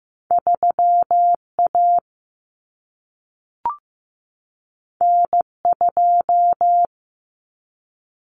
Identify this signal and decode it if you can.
Morse code, heard as a keyed tone, about 15 wpm, 700 Hz. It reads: 3A N2
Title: Common Cut Numbers - 2-Digit 15wpm